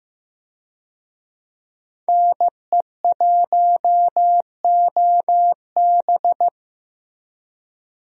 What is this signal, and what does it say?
Morse code, heard as a keyed tone, about 15 wpm, 700 Hz. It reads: NE1OB
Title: USA Call Signs - 5 Characters - Repeated 2x - 15wpm